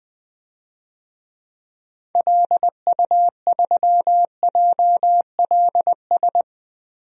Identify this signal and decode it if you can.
Morse code, heard as a keyed tone, about 21 wpm, 700 Hz. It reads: LU3JLS
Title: DXCC-100 Call Signs - 6 Characters - Repeated 3x - 20wpm